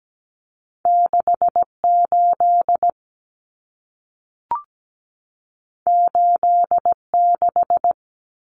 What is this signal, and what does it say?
Morse code, heard as a keyed tone, about 17 wpm, 700 Hz. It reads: 6 8 8 6
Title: Common Cut Numbers - 2-Digit 17wpm